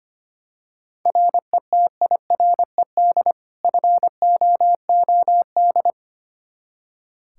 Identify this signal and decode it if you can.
Morse code, heard as a keyed tone, about 25 wpm, 700 Hz. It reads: RETIRED FOOD
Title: K1USN SST QSO Practice 25wpm